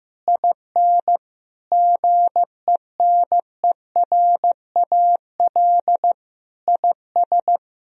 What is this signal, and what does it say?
Morse code, heard as a keyed tone, about 15 wpm, 700 Hz. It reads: IN GENERAL IS
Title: Sentences from Top 1000 Words - Review Part 3 - 15wpm